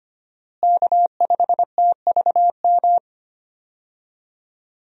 Morse code, heard as a keyed tone, about 25 wpm, 700 Hz. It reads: K5TVM
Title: USA Call Signs - 5 Characters - Repeated 2x - 25wpm